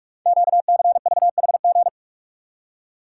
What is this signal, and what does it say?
Morse code, heard as a keyed tone, about 44 wpm, 700 Hz. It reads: QCVHG
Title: Code Groups - 5 Letters - 45wpm